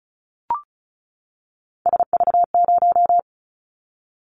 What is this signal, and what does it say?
Morse code, heard as a keyed tone, about 36 wpm, 690 Hz. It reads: SV0
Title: Callsign Prefixes for Top 50 DXCC - 35wpm